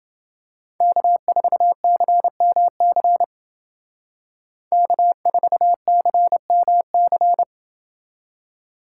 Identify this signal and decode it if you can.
Morse code, heard as a keyed tone, about 29 wpm, 700 Hz. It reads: K4CMC K4CMC